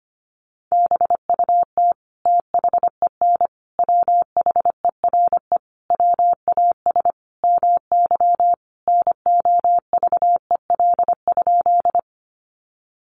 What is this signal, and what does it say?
Morse code, heard as a keyed tone, about 25 wpm, 700 Hz. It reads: BUT THEN WHERE WAS MY NOVEL?
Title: Sentences from Top 1300 Words 25wpm